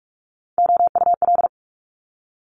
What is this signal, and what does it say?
Morse code, heard as a keyed tone, about 46 wpm, 700 Hz. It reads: OUL
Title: Multi-letter Phonemes 45wpm